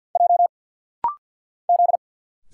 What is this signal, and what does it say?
Morse code, heard as a keyed tone, about 50 wpm, 690 Hz. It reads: J Z
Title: Single Letters - Rapid-Fire 50wpm